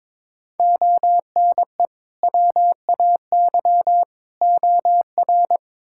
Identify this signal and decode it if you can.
Morse code, heard as a keyed tone, about 21 wpm, 700 Hz. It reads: ONE WAY OR
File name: practice / 22wpm / Build A Phrase - 5-Word Phrases - 22wpm